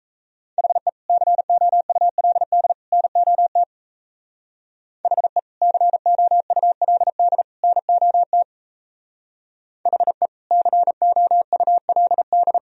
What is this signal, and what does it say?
Morse code, heard as a keyed tone, about 37 wpm, 700 Hz. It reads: HE COULD NOT HE COULD NOT HE COULD